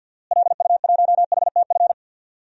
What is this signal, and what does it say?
Morse code, heard as a keyed tone, about 53 wpm, 700 Hz. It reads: LU1HTF
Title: DXCC-100 Call Signs - 6 Characters - Repeated 2x - 50wpm